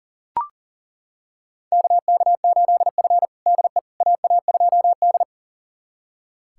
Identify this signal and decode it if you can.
Morse code, heard as a keyed tone, about 37 wpm, 700 Hz. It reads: KK8F DE AA2D
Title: USA Callsign DE Callsign - 4 Characters - 40wpm